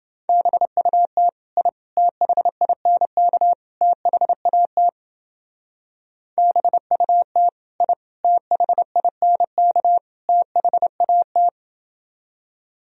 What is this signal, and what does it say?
Morse code, heard as a keyed tone, about 31 wpm, 700 Hz. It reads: BUT I THINK THAT BUT I THINK THAT